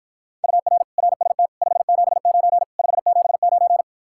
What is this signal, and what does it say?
Morse code, heard as a keyed tone, about 54 wpm, 700 Hz. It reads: UR RST 579 579